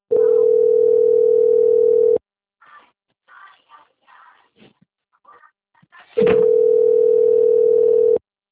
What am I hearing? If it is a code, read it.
Ringback tone (North America), 440 + 480 Hz.